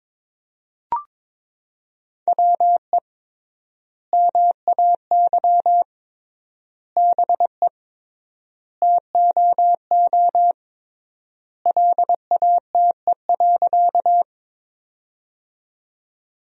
Morse code, heard as a keyed tone, about 22 wpm, 700 Hz. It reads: WE MAY BE TOO LATE.
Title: 3x Word-Spacing - Sentences from Top 300 Words - 22wpm